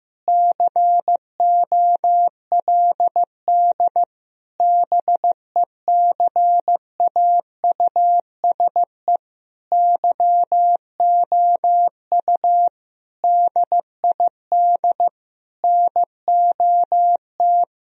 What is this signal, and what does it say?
Morse code, heard as a keyed tone, about 15 wpm, 700 Hz. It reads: COLD BECAUSE YOU DID NOT
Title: Sentences from Top 1000 Words - Review Part 3 - 15wpm